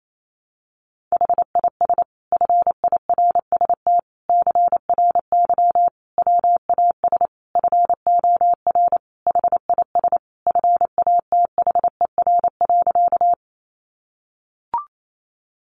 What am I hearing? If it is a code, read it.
Morse code, heard as a keyed tone, about 28 wpm, 700 Hz. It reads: HIS FIRST CRY WAS FOR HIS FATHER.
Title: Sentences from Top 500 Words 28wpm